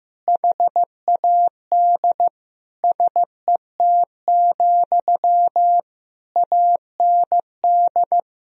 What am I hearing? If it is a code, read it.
Morse code, heard as a keyed tone, about 15 wpm, 700 Hz. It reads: HAD SET, AND
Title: Sentences from Top 1000 Words - Review Part 1 - 15wpm